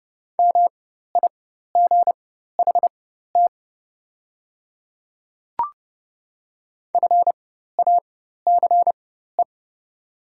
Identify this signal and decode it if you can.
Morse code, heard as a keyed tone, about 30 wpm, spaced as for 14 wpm, 700 Hz. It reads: MIGHT FACE
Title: 4x Character Spacing - Top 300 Words - 30wpm